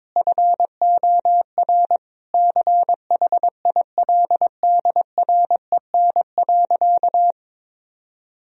Morse code, heard as a keyed tone, about 22 wpm, 700 Hz. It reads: FOR CHILDREN.
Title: Sentences from Top 1000 Words - Review Part 4 - 22wpm